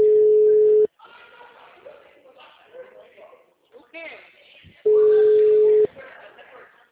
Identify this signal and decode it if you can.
Ringback tone (Europe, 425 Hz), with speech.
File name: out- -2004- - - .11